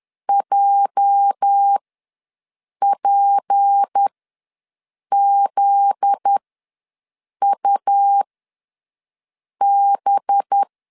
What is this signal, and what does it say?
Morse code, heard as a keyed tone, about 10 wpm, spaced as for 6 wpm, 800 Hz. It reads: JPZUB
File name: LEZIONI / 9 2 ws 10wpm 4130